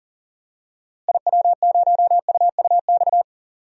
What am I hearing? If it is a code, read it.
Morse code, heard as a keyed tone, about 40 wpm, 700 Hz. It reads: IW0UUX
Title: DXCC-100 Call Signs - 6 Characters - Repeated 2x - 40wpm